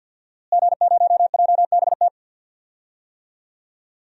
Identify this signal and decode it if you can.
Morse code, heard as a keyed tone, about 51 wpm, 700 Hz. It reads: G0JBT
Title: DXCC-100 Call Signs - 5 Characters - Repeated 3x - 50wpm